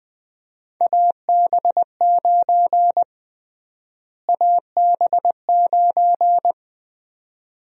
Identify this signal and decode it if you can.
Morse code, heard as a keyed tone, about 20 wpm, 700 Hz. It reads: AB9 AB9
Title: USA Call Sign Prefixes - Repeated 2x - 20wpm